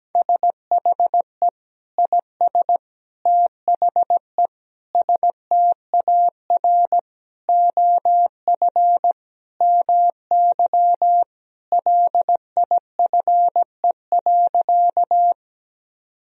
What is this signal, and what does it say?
Morse code, heard as a keyed tone, about 17 wpm, 700 Hz. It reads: SHE IS THE STAR OF MY LIFE.
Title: Sentences from Top 1000 Words - Review Part 2 - 17wpm